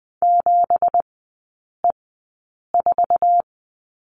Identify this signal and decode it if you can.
Morse code, heard as a keyed tone, about 20 wpm, spaced as for 8 wpm, 690 Hz. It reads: 7E4
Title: 172 Lesson 28e - Number 9 - Three Characters 20wpm